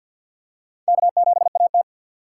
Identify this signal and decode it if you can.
Morse code, heard as a keyed tone, about 49 wpm, 700 Hz. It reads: K7AT
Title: USA Call Signs - 4 Characters - 2x Character Spacing - 50wpm